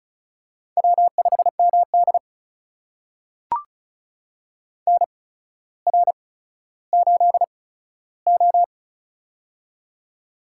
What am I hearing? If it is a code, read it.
Morse code, heard as a keyed tone, about 36 wpm, spaced as for 15 wpm, 700 Hz. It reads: W5MD NR8O
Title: USA Call Signs - 4 Characters - 8x Character Spacing - 35wpm